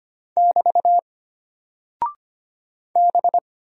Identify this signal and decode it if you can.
Morse code, heard as a keyed tone, about 25 wpm, 700 Hz. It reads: <BT> B
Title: Single Character (Letter:Number:Punctuation:Prosign) Alt 25wpm